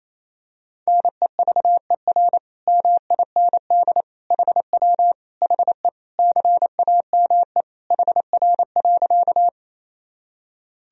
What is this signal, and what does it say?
Morse code, heard as a keyed tone, about 28 wpm, 700 Hz. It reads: NEVER MIND HW HE CAME HR.
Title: Sentences from Top 500 Words and Abbreviations 28wpm